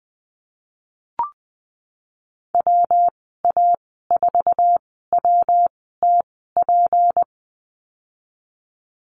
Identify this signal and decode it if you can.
Morse code, heard as a keyed tone, about 20 wpm, spaced as for 15 wpm, 700 Hz. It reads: WA4WTP
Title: 231 Lesson 37g - Letter x - Call Signs 20wpm